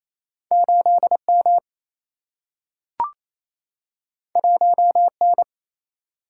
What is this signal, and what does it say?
Morse code, heard as a keyed tone, about 28 wpm, 700 Hz. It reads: 8 M 1 N
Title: Single-Letter Single-Number 28wpm